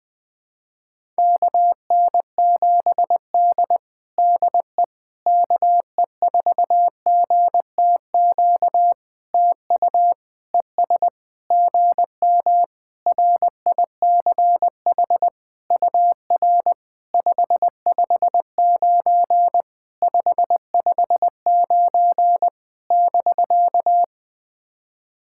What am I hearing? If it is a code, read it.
Morse code, heard as a keyed tone, about 20 wpm, 700 Hz. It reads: KN7D DE KE4GTQ TU ES GM RICH UR 559 559 <BK>
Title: POTA QSO Practice - 20wpm